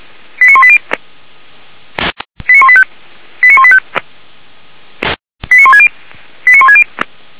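Five-tone selective call (ZVEI), heard as quick runs of five tones, tones of about 70 ms. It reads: 89180 89185 89185 89160 89160
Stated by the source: ZVEI five-tone selective call received on SDR.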